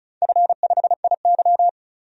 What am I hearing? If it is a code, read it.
Morse code, heard as a keyed tone, about 36 wpm, 700 Hz. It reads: F5IY
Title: DXCC-100 Call Signs - 4 Characters - 35wpm